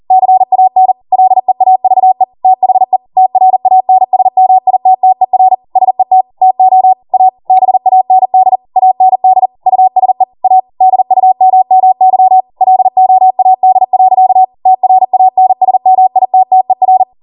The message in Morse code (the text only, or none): CAN LEAVE THE TRANSMITTER SET TO A BAND AND USE A DUMMY LOAD. TRANSMITTER